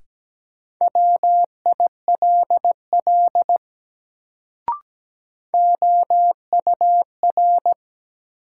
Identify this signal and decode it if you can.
Morse code, heard as a keyed tone, about 17 wpm, 700 Hz. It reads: WILL OUR
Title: Sets of 2 Words - New Word Drill - 17wpm